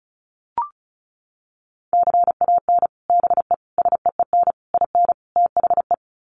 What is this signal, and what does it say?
Morse code, heard as a keyed tone, about 34 wpm, 700 Hz. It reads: CAN BE SEEN IN THE
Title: Sets of 5 Words Encore2 35wpm